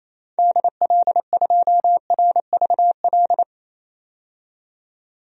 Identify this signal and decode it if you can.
Morse code, heard as a keyed tone, about 28 wpm, 700 Hz. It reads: DL2RVL
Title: DXCC-100 Call Signs - 6 Characters - 28wpm